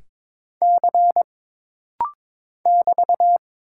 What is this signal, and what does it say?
Morse code, heard as a keyed tone, about 23 wpm, 700 Hz. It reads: C <BT>
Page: This rapid-fire timing is geared to push your ICR (Instant Character Recognition) skills.